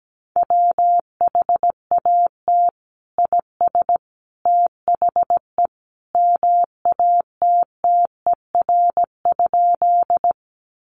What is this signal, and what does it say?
Morse code, heard as a keyed tone, about 17 wpm, 700 Hz. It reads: WHAT IS THE MATTER?
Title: Sentences from Top 700 Words 17wpm